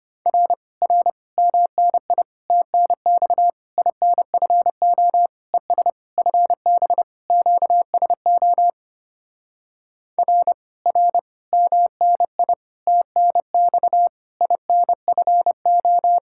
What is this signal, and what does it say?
Morse code, heard as a keyed tone, about 29 wpm, 700 Hz. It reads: R R MNI TNX INFO ES FB QSO R R MNI TNX INFO
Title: Speed-Racing Common QSO Phrases Encore - 25wpm